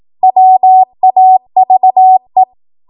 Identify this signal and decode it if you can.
Morse code, heard as a keyed tone, about 18 wpm, 750 Hz. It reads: WAVE
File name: Morse / 18WPM